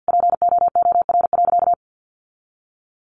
Morse code, heard as a keyed tone, about 50 wpm, 700 Hz. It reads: POOR.
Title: Sentences from Top 1100 Words 50wpm